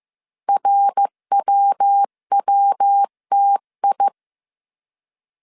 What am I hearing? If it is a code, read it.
Morse code, heard as a keyed tone, about 15 wpm, 800 Hz. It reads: RWWTI
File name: CW / DET / 1 3 1500C S C 14wpm 2458